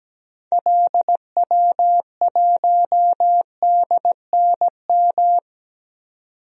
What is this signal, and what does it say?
Morse code, heard as a keyed tone, about 17 wpm, 700 Hz. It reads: LW1DNM
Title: DXCC-100 Call Signs - 6 Characters - 17wpm